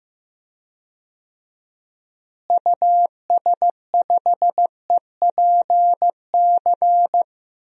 Morse code, heard as a keyed tone, about 15 wpm, 700 Hz. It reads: US5EPC
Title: DXCC-100 Call Signs - 6 Characters - Repeated 3x - 15wpm